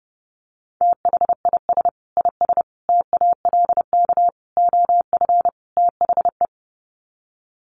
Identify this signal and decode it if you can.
Morse code, heard as a keyed tone, about 30 wpm, 700 Hz. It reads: THIS IS TALK OF THE
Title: Sets of 5 Words 30wpm